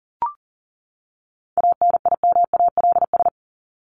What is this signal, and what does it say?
Morse code, heard as a keyed tone, about 40 wpm, 700 Hz. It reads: ANIMALS
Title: ES:S Suffix - Top 100 Words with Base Words 40wpm